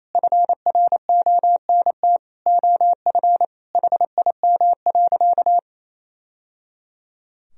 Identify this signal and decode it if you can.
Morse code, heard as a keyed tone, about 28 wpm, 700 Hz. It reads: FRONT OF HIM.
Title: Sentences from Top 1000 Words - Review Part 2 - 28wpm